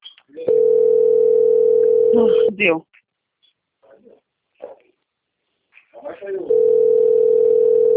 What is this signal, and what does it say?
Ringback tone (North America), 440 + 480 Hz, with speech.
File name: out- - - - - .25715